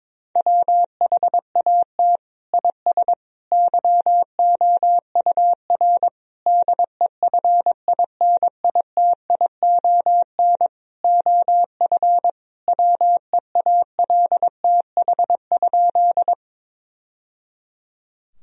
Morse code, heard as a keyed tone, about 22 wpm, 700 Hz. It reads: WHAT IS YOUR DEFINITION OF WEALTH?
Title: Sentences from Top 1000 Words - Review Part 4 - 22wpm